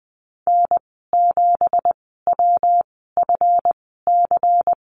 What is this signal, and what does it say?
Morse code, heard as a keyed tone, about 20 wpm, spaced as for 15 wpm, 700 Hz. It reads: N7WFC
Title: 173 Lesson 28g - Number 9 - Call Signs 20wpm